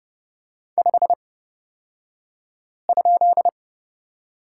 Morse code, heard as a keyed tone, about 31 wpm, spaced as for 5 wpm, 690 Hz. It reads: 5?
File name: Lesson 36d - BT - Two Characters 30wpm